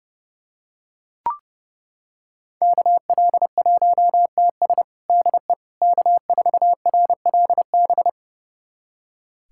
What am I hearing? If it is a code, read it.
Morse code, heard as a keyed tone, about 30 wpm, 700 Hz. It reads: KL1TS DE K4RLB
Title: USA Callsign DE Callsign 30wpm